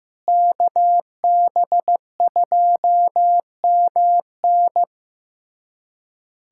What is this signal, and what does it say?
Morse code, heard as a keyed tone, about 15 wpm, 700 Hz. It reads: KB2MN
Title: USA Callsign DE Callsign - 5 Characters - 15wpm